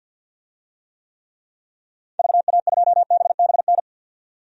Morse code, heard as a keyed tone, about 50 wpm, 700 Hz. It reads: VA2BBN
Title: DXCC-100 Call Signs - 6 Characters - 50wpm